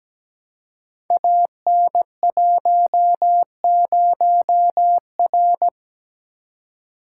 Morse code, heard as a keyed tone, about 17 wpm, 700 Hz. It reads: AN10R
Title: Code Groups - 5 Letter:Numbers - 17wpm